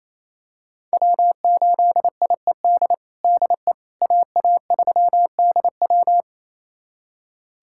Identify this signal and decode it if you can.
Morse code, heard as a keyed tone, about 28 wpm, 700 Hz. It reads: W8IED DE AA3DW
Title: USA Callsign DE Callsign - 5 Characters - 28wpm